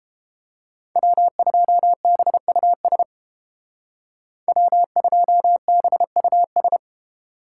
Morse code, heard as a keyed tone, about 31 wpm, 700 Hz. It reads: W2BUS W2BUS